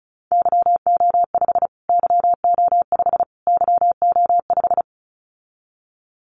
Morse code, heard as a keyed tone, about 34 wpm, 700 Hz. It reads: YO5 YO5 YO5
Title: Callsign Prefixes for Top 25 DXCC - Repeated 3x - 35wpm